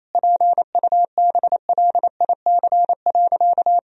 Morse code, heard as a keyed tone, about 27 wpm, 700 Hz. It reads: PUBLIC.
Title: Sentences from Top 1000 Words - Review Part 3 - 28wpm